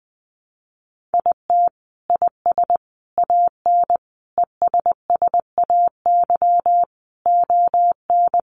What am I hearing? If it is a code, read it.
Morse code, heard as a keyed tone, about 20 wpm, 700 Hz. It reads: IT IS AN ESSAY ON
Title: Sentences from Top 1500 Words 20wpm